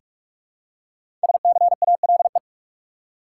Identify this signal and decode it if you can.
Morse code, heard as a keyed tone, about 43 wpm, 700 Hz. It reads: SCALE
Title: Five Letter Words - 45wpm